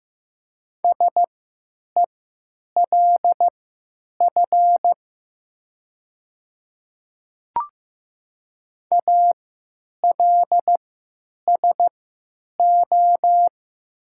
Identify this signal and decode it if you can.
Morse code, heard as a keyed tone, about 15 wpm, spaced as for 8 wpm, 700 Hz. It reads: SELF ALSO
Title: 3x Character Spacing - Top 200 Words - 15wpm